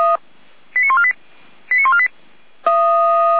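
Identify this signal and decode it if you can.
Five-tone selective call (ZVEI), heard as quick runs of five tones, tones of about 80 ms. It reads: 89148 89148
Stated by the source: ZVEI five-tone selective call received on SDR.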